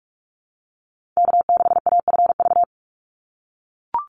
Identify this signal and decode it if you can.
Morse code, heard as a keyed tone, about 44 wpm, 700 Hz. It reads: K6AFV
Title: USA Call Signs 45wpm